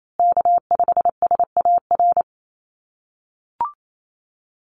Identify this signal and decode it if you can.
Morse code, heard as a keyed tone, about 28 wpm, 700 Hz. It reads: K5SAR
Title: USA Call Signs Repeated 3x 28wpm